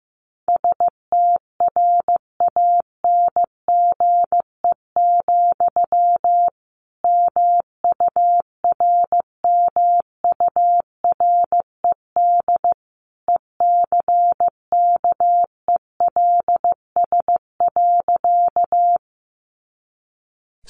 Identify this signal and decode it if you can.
Morse code, heard as a keyed tone, about 15 wpm, 700 Hz. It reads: STRANGE, MURMURED ECKELS.
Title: A Sound Of Thunder 15wpm